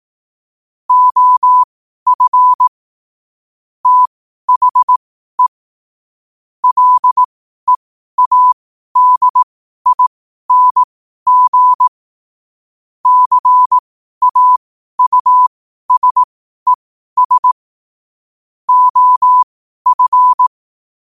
Morse code, heard as a keyed tone, about 18 wpm, spaced as for 12 wpm, 1000 Hz. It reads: OF THE LEADING CAUSES OF
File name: qotdfiles / 12wpm